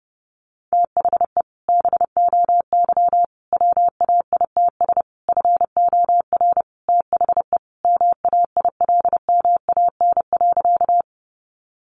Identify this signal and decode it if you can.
Morse code, heard as a keyed tone, about 30 wpm, 700 Hz. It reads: THE BOY WAITS FOR THE MAILMAN.
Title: Additional Easy Phrases 30wpm